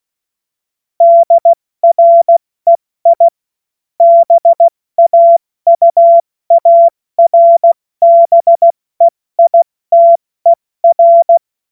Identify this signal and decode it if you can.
Morse code, heard as a keyed tone, about 16 wpm, 680 Hz. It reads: DREI BAUARBEITER
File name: Morsekurs / Witze / Witz 01. November 2022 70 F80